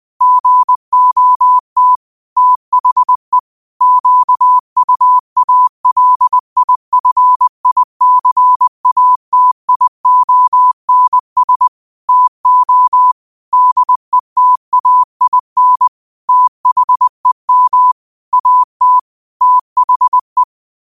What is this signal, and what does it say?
Morse code, heard as a keyed tone, about 20 wpm, 1000 Hz. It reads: GOT THE QUALIFICATIONS TO DETAIN THEM AT THE